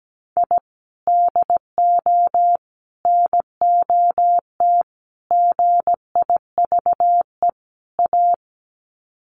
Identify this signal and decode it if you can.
Morse code, heard as a keyed tone, about 17 wpm, 700 Hz. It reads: I DO NOT GIVE A
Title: Sets of 5 Words Encore2 17wpm